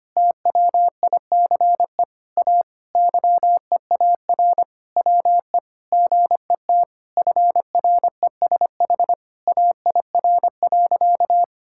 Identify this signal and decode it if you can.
Morse code, heard as a keyed tone, about 25 wpm, 700 Hz. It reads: TWICE A YEAR WE GET FRESH AIR.
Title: Sentences from Top 1600 Words 25wpm